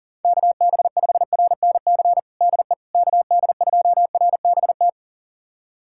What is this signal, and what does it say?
Morse code, heard as a keyed tone, about 40 wpm, 690 Hz. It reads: KB5RNC DE KD2RBT
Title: USA Callsign DE Callsign - 6 Characters - 40wpm